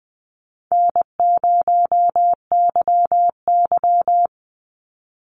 Morse code, heard as a keyed tone, about 20 wpm, 700 Hz. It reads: N0YY
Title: Easy Call Signs Repeated 2x - 20wpm